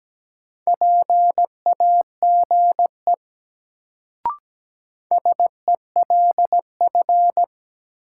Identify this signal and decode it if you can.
Morse code, heard as a keyed tone, about 17 wpm, 700 Hz. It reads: PAGE SELF
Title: Sentences from Top 1000 Words - Review Part 1 - New Word Drill - 17wpm